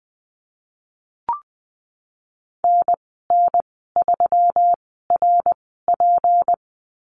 Morse code, heard as a keyed tone, about 20 wpm, spaced as for 15 wpm, 700 Hz. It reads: NN3RP
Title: 154 Lesson 25g - Letter p - Call Signs 20wpm